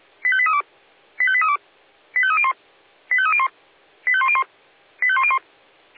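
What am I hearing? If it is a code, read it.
Five-tone selective call (ZVEI), heard as quick runs of five tones, tones of about 70 ms. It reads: 85832 85832 85301 85301 85101 85101
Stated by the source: ZVEI five-tone call received on SDR.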